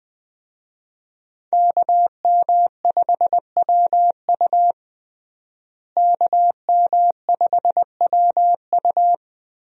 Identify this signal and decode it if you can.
Morse code, heard as a keyed tone, about 20 wpm, 700 Hz. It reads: KM5WU KM5WU
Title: USA Call Signs - 5 Characters - Repeated 2x - 20wpm